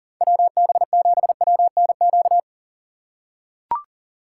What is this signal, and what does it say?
Morse code, heard as a keyed tone, about 40 wpm, 700 Hz. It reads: WB7WNQ
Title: USA Call Signs - 6 Characters - 3x Character Spacing - 40wpm